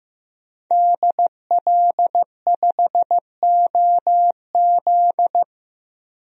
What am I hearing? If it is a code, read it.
Morse code, heard as a keyed tone, about 15 wpm, 700 Hz. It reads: DL5OZ
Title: DXCC-100 Call Signs - 5 Characters - 15wpm